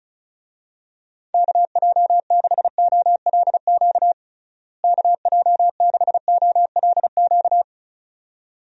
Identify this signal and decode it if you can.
Morse code, heard as a keyed tone, about 36 wpm, 700 Hz. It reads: KJ6OLQ KJ6OLQ
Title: USA Call Signs - 6 Characters - Repeated 2x - 35wpm